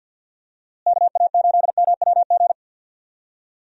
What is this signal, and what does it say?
Morse code, heard as a keyed tone, about 51 wpm, 700 Hz. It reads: KA8MWG
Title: USA Call Signs - 6 Characters - Repeated 2x - 50wpm